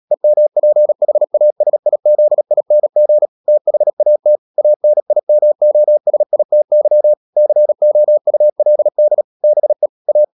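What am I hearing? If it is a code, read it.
Morse code, heard as a keyed tone, about 37 wpm, 600 Hz. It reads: EMPHASIZING THAT ANIMOSITY COULD BE A
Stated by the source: Morse code practice at 35–39 wpm sending English text of Ken Follett's novel Never, from CW med Gnister.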